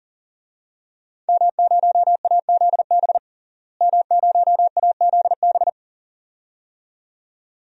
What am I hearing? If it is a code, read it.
Morse code, heard as a keyed tone, about 40 wpm, 700 Hz. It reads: M0AZB M0AZB
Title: DXCC-100 Call Signs - 5 Characters - Repeated 2x - 40wpm